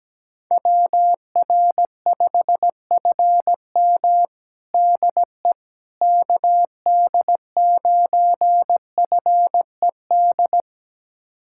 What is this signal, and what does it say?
Morse code, heard as a keyed tone, about 17 wpm, 700 Hz. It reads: WR5FM DE KD9FED
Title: USA Callsign DE Callsign 17wpm